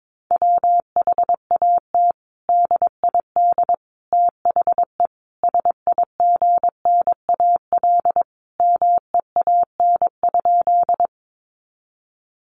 Morse code, heard as a keyed tone, about 22 wpm, 700 Hz. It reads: WHAT DID THE SIGNAL MEAN?
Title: Sentences from Top 1400 Words 22wpm